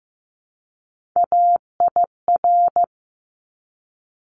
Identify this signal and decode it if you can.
Morse code, heard as a keyed tone, about 15 wpm, 700 Hz. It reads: AIR
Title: Top 200 Words 15wpm